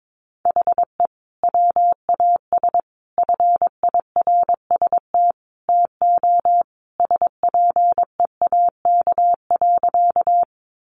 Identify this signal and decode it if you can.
Morse code, heard as a keyed tone, about 22 wpm, 700 Hz. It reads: HE WAS FIRST TO SPEAK.
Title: Sentences from Top 1100 Words 22wpm